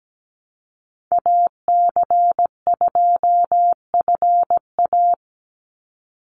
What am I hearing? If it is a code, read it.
Morse code, heard as a keyed tone, about 17 wpm, 700 Hz. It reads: AC2FA
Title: Call Signs - 17wpm